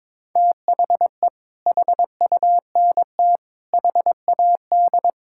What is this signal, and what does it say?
Morse code, heard as a keyed tone, about 22 wpm, 700 Hz. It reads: THE HUNT HAD
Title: Sentences from Top 1600 Words 22wpm